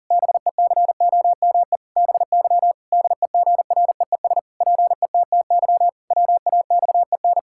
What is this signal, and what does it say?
Morse code, heard as a keyed tone, about 37 wpm, 700 Hz. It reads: BECOME BY DEGREES PETTY WAXEN
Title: Pangram Challenge v2 - 40wpm